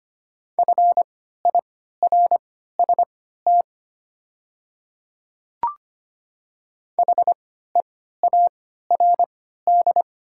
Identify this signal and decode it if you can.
Morse code, heard as a keyed tone, about 25 wpm, spaced as for 14 wpm, 700 Hz. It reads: FIRST HEARD
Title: 3x Character Spacing - Top 500 Words - 25wpm